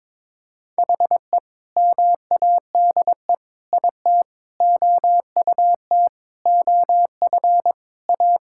Morse code, heard as a keyed tone, about 22 wpm, 700 Hz. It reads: HE MADE IT OUT OF A
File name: practice / 22wpm / Sentences with 3-Word Phrases - 22wpm